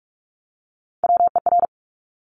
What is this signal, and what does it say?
Morse code, heard as a keyed tone, about 46 wpm, 700 Hz. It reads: WER
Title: Top Three Letter Combinations in Words 45wpm